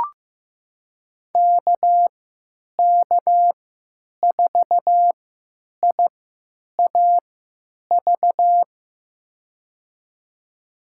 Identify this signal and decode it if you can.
Morse code, heard as a keyed tone, about 15 wpm, spaced as for 9 wpm, 700 Hz. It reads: KK4IAV
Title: USA Call Signs - 6 Characters - 3x Character Spacing - 15wpm